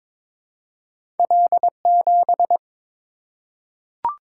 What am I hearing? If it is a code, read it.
Morse code, heard as a keyed tone, about 23 wpm, 700 Hz. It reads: L7
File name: practice / 22wpm / Two Letter-Number Combinations - 22wpm